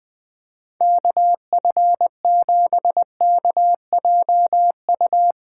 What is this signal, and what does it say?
Morse code, heard as a keyed tone, about 20 wpm, 700 Hz. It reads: KF7KJU
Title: Speed Racing - USA Call Signs 17wpm